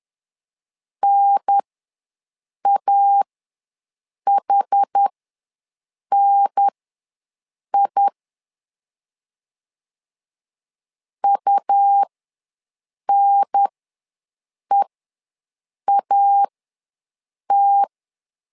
Morse code, heard as a keyed tone, about 10 wpm, spaced as for 6 wpm, 800 Hz. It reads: NAHNI UNEAT
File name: LEZIONI / 2 UHDN / 2 UHDN ws 10wpm 2044